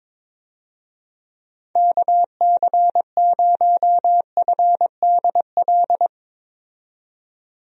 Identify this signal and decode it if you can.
Morse code, heard as a keyed tone, about 22 wpm, 700 Hz. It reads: KC0FDL
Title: USA Call Signs - 6 Characters - 22wpm